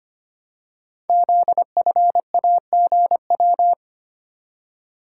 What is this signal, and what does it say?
Morse code, heard as a keyed tone, about 25 wpm, 700 Hz. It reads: ZFAGW
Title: Code Groups - 5 Letters - 25wpm